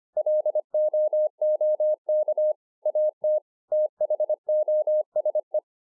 Morse code, heard as a keyed tone, about 25 wpm, 610 Hz. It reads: LOOK AT THOSE